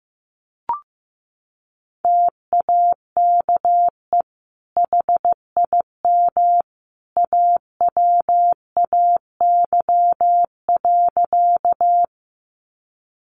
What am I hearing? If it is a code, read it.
Morse code, heard as a keyed tone, about 15 wpm, 700 Hz. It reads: TAKE HIM AWAY.
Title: Sentences from Top 600 Words 15wpm